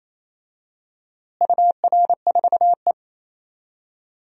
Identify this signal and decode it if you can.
Morse code, heard as a keyed tone, about 28 wpm, 700 Hz. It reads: UR4E
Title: DXCC-100 Call Signs - 4 Characters - Repeated 2x - 28wpm